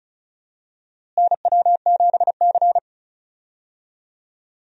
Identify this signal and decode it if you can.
Morse code, heard as a keyed tone, about 34 wpm, 700 Hz. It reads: NW7C
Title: USA Call Signs - 4 Characters - Repeated 3x - 35wpm